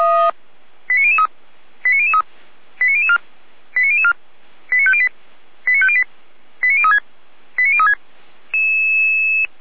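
Five-tone selective call (ZVEI), heard as quick runs of five tones, tones of about 70 ms. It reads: 89003 89003 89004 89004 89508 89508 89036 89036
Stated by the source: ZVEI five-tone call received on SDR by Feuerwehr Orlamünde.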